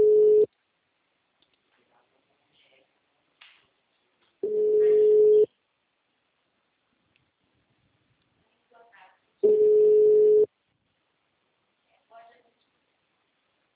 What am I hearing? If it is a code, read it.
Ringback tone (Europe, 425 Hz), with speech.
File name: out- -2006- - - .203